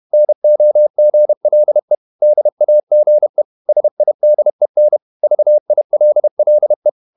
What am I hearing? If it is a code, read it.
Morse code, heard as a keyed tone, about 31 wpm, 600 Hz. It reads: NOGLE DAGE SIDEN VILLE
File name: Ferie QRQ 31wpm